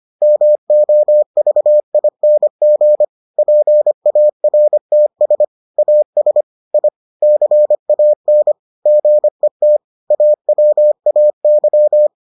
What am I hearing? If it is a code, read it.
Morse code, heard as a keyed tone, about 25 wpm, 600 Hz. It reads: MOVING PARTS AS I CAN GET AWAY